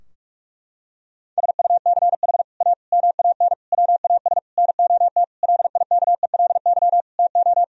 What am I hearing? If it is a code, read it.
Morse code, heard as a keyed tone, about 46 wpm, 700 Hz. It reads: SUCH A MAN WAS NOT LIKELY TO
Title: Sentences from Top 1000 Words - Review Part 4 - 45wpm